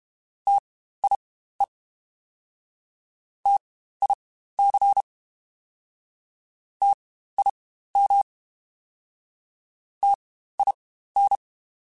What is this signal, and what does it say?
Morse code, heard as a keyed tone, about 32 wpm, spaced as for 15 wpm, 800 Hz. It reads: TIE TIC TIM TIN
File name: Spaced Alphas / Spaced Alphas mp3s / Ts sp @30